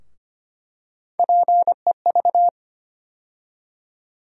Morse code, heard as a keyed tone, about 25 wpm, 700 Hz. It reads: PEV
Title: USA Call Sign Postfixes - 3 Letters 25wpm